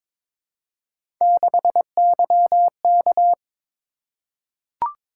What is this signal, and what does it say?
Morse code, heard as a keyed tone, about 23 wpm, 700 Hz. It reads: 6YK
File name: practice / 22wpm / Three Letter-Number Combinations - 22wpm